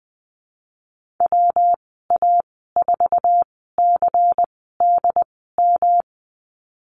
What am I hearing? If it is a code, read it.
Morse code, heard as a keyed tone, about 20 wpm, spaced as for 15 wpm, 700 Hz. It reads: WA4CDM
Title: 186 Lesson 30g - Letter b - Call Signs 20wpm